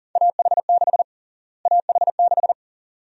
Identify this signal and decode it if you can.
Morse code, heard as a keyed tone, about 40 wpm, 700 Hz. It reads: AH6 AH6
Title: USA Call Sign Prefixes - Repeated 2x - 40wpm